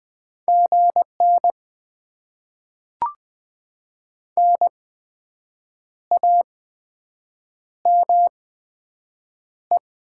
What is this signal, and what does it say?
Morse code, heard as a keyed tone, about 20 wpm, spaced as for 7 wpm, 700 Hz. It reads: GN NAME
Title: Top 90 QSO Elements - 8x Character Spacing - 20wpm